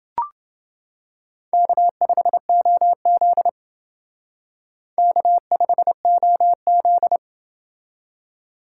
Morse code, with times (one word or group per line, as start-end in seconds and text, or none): 1.5-3.5 K5OZ
5.0-7.2 K5OZ